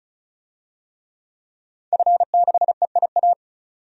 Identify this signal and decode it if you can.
Morse code, heard as a keyed tone, about 36 wpm, 700 Hz. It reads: F6EIA
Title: DXCC-100 Call Signs - 5 Characters - 35wpm